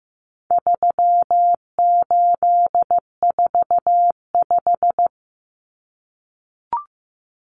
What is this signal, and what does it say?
Morse code, heard as a keyed tone, about 15 wpm, 700 Hz. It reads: 3845
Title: Common Cut Numbers - 4-Digit 15wpm